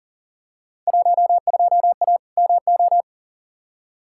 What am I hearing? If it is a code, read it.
Morse code, heard as a keyed tone, about 40 wpm, 700 Hz. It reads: 12A MO
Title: Field Day Exchanges Encore - 40wpm